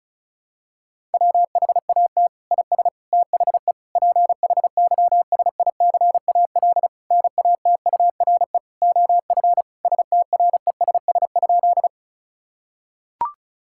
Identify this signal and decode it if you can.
Morse code, heard as a keyed tone, about 36 wpm, 700 Hz. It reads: WHAT IS THE PHYSICAL NATURE OF STRESS?
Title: Sentences from Top 1000 Words - Review Part 6 - 35wpm